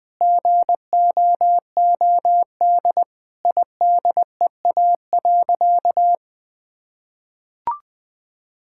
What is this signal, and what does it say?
Morse code, heard as a keyed tone, about 21 wpm, 700 Hz. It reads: GOOD IDEA.
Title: Sentences with 5-Word Phrases - 20wpm